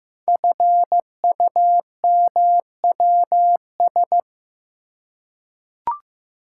Morse code, heard as a keyed tone, about 15 wpm, 700 Hz. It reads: FUMWS
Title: Code Groups - 5 Letters - 15wpm